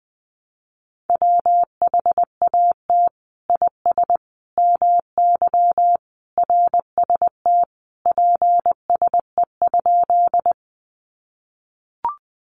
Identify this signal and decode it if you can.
Morse code, heard as a keyed tone, about 20 wpm, 700 Hz. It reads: WHAT IS MY RST PSE?
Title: Common QSO Phrases 20wpm